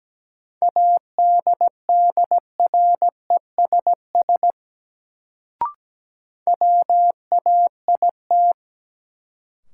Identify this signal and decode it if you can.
Morse code, heard as a keyed tone, about 17 wpm, 700 Hz. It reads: ADDRESS WAIT
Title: Sets of 5 Words Encore - New Word Drill - 17wpm